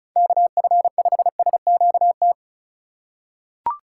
Morse code, heard as a keyed tone, about 36 wpm, 700 Hz. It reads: KF5SQT
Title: USA Call Signs - 6 Characters - 3x Character Spacing - 35wpm